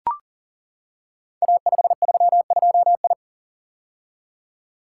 Morse code, heard as a keyed tone, about 38 wpm, 690 Hz. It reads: A532I
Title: Code Groups - 5 Letter:Numbers - 40wpm